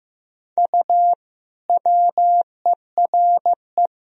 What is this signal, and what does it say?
Morse code, heard as a keyed tone, about 15 wpm, 700 Hz. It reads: U WERE
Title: Sentences from Top 400 Words and Abbreviations 15wpm